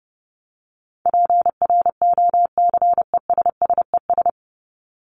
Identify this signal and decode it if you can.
Morse code, heard as a keyed tone, about 30 wpm, 700 Hz. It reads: PROCESSES
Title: ES:S Suffix - Top 100 Words with Base Words 30wpm